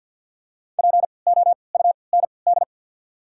Morse code, heard as a keyed tone, about 51 wpm, spaced as for 28 wpm, 690 Hz. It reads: POUND
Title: 3x Character Spacing - Top 500 Words - 50wpm